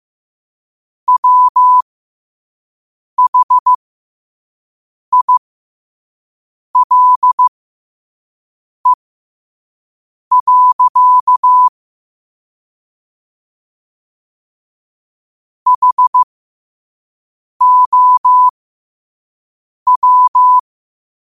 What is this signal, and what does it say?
Morse code, heard as a keyed tone, about 15 wpm, spaced as for 5 wpm, 1000 Hz. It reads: WHILE. HOW